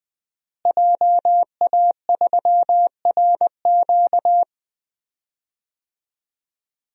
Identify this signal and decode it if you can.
Morse code, heard as a keyed tone, about 20 wpm, 700 Hz. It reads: JA3RQ
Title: DXCC-100 Call Signs - 5 Characters - Repeated 3x - 20wpm